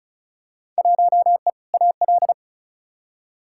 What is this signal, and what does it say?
Morse code, heard as a keyed tone, about 34 wpm, 700 Hz. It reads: 1E AL
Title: Field Day Exchanges - 35wpm